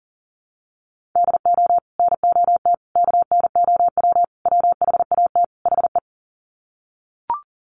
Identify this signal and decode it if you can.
Morse code, heard as a keyed tone, about 40 wpm, 700 Hz. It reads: DO NOT KNOW WHAT HE
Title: Sets of 5 Words Encore2 40wpm